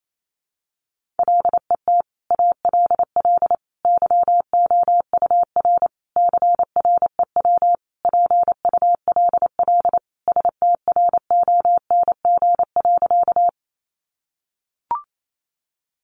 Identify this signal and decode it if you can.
Morse code, heard as a keyed tone, about 28 wpm, 700 Hz. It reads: LET ALL YOUR CREW PULL STRONG.
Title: Sentences from Top 1500 Words 28wpm